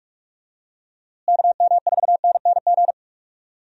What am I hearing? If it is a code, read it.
Morse code, heard as a keyed tone, about 46 wpm, 700 Hz. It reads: KM4NNG
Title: USA Call Signs - 6 Characters - Repeated 2x - 45wpm